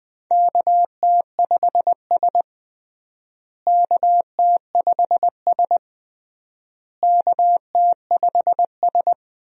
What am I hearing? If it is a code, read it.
Morse code, heard as a keyed tone, about 20 wpm, 700 Hz. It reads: KT5S KT5S KT5S
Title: USA Call Signs - 4 Characters - Repeated 3x - 20wpm